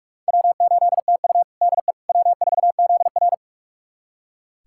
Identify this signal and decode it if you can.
Morse code, heard as a keyed tone, about 41 wpm, 700 Hz. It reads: W8TU DE W4ZR
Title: USA Callsign DE Callsign - 4 Characters - 45wpm